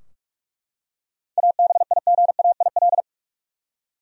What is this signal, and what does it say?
Morse code, heard as a keyed tone, about 41 wpm, 700 Hz. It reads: ABIGAIL
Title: USA Common Names - 45wpm